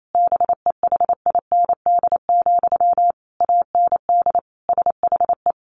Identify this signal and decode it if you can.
Morse code, heard as a keyed tone, about 28 wpm, 700 Hz. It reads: BEHIND, AND SHE
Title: Sentences from Top 1500 Words 28wpm